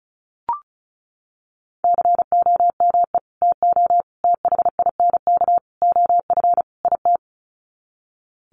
Morse code, heard as a keyed tone, about 34 wpm, 700 Hz. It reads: COME TO THINK OF IT
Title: Sets of 5 Words Encore 35wpm